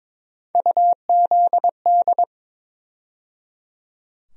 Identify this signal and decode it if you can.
Morse code, heard as a keyed tone, about 22 wpm, 700 Hz. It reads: UZD
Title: USA Call Sign Postfixes - 3 Letters 22wpm